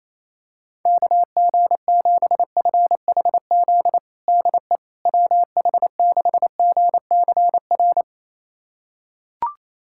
Morse code, heard as a keyed tone, about 28 wpm, 700 Hz. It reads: KG7FHZ DE WH6GCR
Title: USA Callsign DE Callsign - 6 Characters - 28wpm